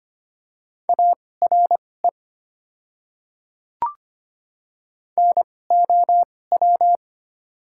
Morse code, heard as a keyed tone, about 25 wpm, spaced as for 18 wpm, 700 Hz. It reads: ARE NOW
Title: 2x Character Spacing - Top 100 Words - 25wpm